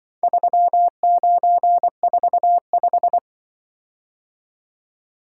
Morse code, heard as a keyed tone, about 23 wpm, 700 Hz. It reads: 3945
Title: Speed-Racing - Four Letter-Number Combinations - 20wpm